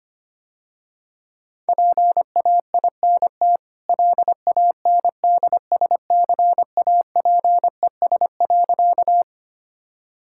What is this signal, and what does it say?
Morse code, heard as a keyed tone, about 25 wpm, 690 Hz. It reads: PAINT LANDSCAPES.
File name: Easy Sentences with Hints - 25wpm